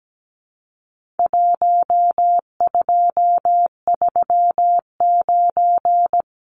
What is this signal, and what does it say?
Morse code, heard as a keyed tone, about 17 wpm, 700 Hz. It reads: 1239
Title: Numbers - 4-Digit 17wpm